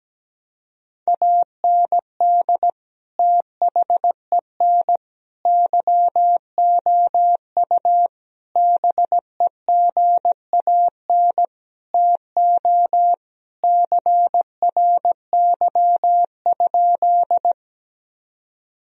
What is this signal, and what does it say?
Morse code, heard as a keyed tone, about 17 wpm, 700 Hz. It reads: AND THEN YOU BEGAN TO CRY?
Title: Sentences from Top 1000 Words - Review Part 2 - 17wpm